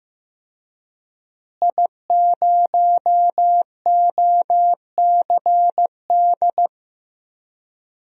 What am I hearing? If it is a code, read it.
Morse code, heard as a keyed tone, about 16 wpm, 700 Hz. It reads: I0OCD
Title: DXCC-100 Call Signs - 5 Characters - Repeated 2x - 15wpm